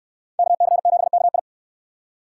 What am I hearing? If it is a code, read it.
Morse code, heard as a keyed tone, about 68 wpm, 700 Hz. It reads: KC7KI